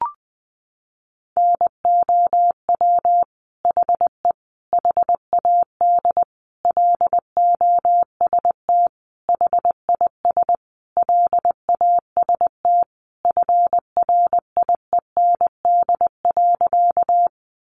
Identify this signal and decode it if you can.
Morse code, heard as a keyed tone, about 20 wpm, 700 Hz. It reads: NOW HE HAD LOST HIS LAST FRIEND.
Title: Sentences from Top 1400 Words 20wpm